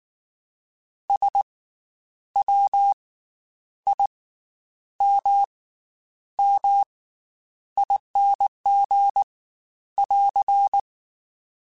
Morse code, heard as a keyed tone, about 19 wpm, spaced as for 9 wpm, 800 Hz. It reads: SWIMMING <AR>
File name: Special INGs / Special INGs-mp3s / Slow ing @18